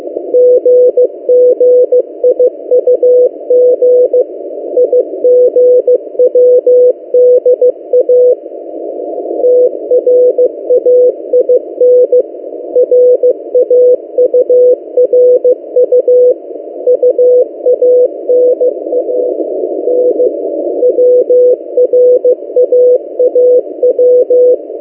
Morse code, heard as a keyed tone, about 15 wpm, 500 Hz. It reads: GGIUG IGWDA TRAIN RAURU UANAN WRAAW
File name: M08a- -1355utc-8096khz